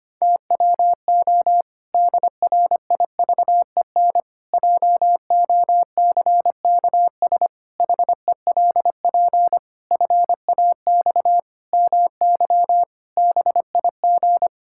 Morse code, heard as a keyed tone, about 25 wpm, 700 Hz. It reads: TWO DRIVEN JOCKS HELP FAX MY BIG
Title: Pangram Challenge v2 - 25wpm